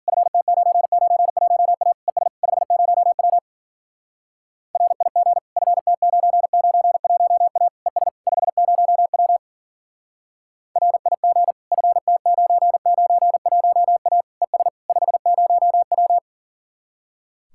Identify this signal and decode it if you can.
Morse code, heard as a keyed tone, about 52 wpm, 700 Hz. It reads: FT991A ES 50W RIG FT991A ES 50W RIG FT991A ES 50W